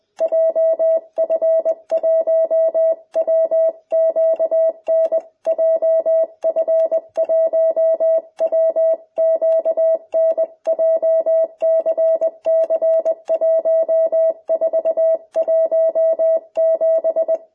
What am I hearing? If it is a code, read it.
Morse code, heard as a keyed tone, about 20 wpm, 650 Hz. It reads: JF1WQN JF1WQN JCC1417